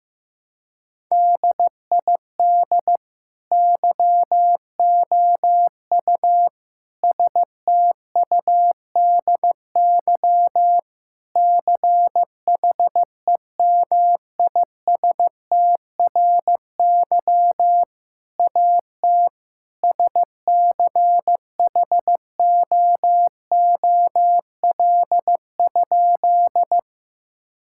Morse code, heard as a keyed tone, about 16 wpm, 700 Hz. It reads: DID YOU STUDY CHEMISTRY AT SCHOOL?
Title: Sentences from Top 1000 Words - Review Part 6 - 15wpm